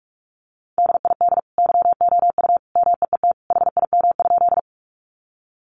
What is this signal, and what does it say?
Morse code, heard as a keyed tone, about 44 wpm, 700 Hz. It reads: DID YOU MEET HIM?
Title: Sentences from Top 1100 Words 45wpm